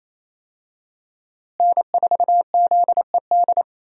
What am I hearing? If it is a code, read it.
Morse code, heard as a keyed tone, about 28 wpm, 700 Hz. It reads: N4ZED
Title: USA Call Signs - 5 Characters - Repeated 3x - 28wpm